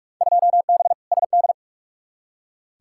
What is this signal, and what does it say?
Morse code, heard as a keyed tone, about 45 wpm, 700 Hz. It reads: 2B SD
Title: Field Day Exchanges Encore - 45wpm